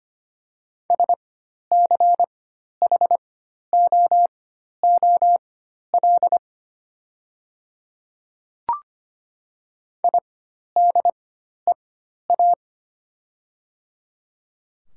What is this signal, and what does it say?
Morse code, heard as a keyed tone, about 25 wpm, spaced as for 12 wpm, 700 Hz. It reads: SCHOOL IDEA
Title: 4x Character Spacing - Top 300 Words - 25wpm